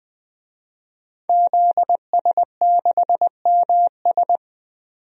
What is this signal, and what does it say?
Morse code, heard as a keyed tone, about 20 wpm, 700 Hz. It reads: ZS6MS
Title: DXCC-100 Call Signs - 5 Characters - 20wpm